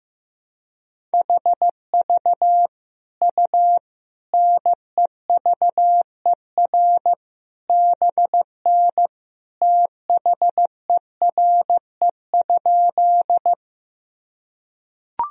Morse code, heard as a keyed tone, about 15 wpm, 700 Hz. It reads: HV U NEVER BN THERE?
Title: Sentences from Top 300 Words and Abbreviations 15wpm